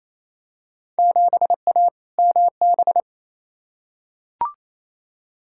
Morse code, heard as a keyed tone, about 28 wpm, 700 Hz. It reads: 7A MB
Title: Field Day Exchanges Encore - 28wpm